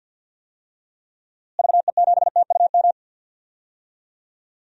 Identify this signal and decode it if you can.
Morse code, heard as a keyed tone, about 50 wpm, 700 Hz. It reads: VE7TUM
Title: DXCC-100 Call Signs - 6 Characters - Repeated 2x - 50wpm